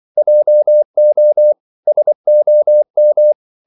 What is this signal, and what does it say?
Morse code, heard as a keyed tone, about 24 wpm, 600 Hz. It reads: JO SOM
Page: Ferie 24 wpm